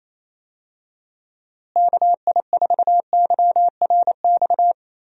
Morse code, heard as a keyed tone, about 28 wpm, 700 Hz. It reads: KI4YRX
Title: USA Call Signs - 6 Characters - Repeated 2x - 28wpm